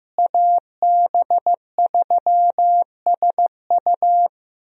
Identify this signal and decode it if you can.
Morse code, heard as a keyed tone, about 15 wpm, 700 Hz. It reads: AB3SU
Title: USA Call Signs - 5 Characters - Repeated 3x - 15wpm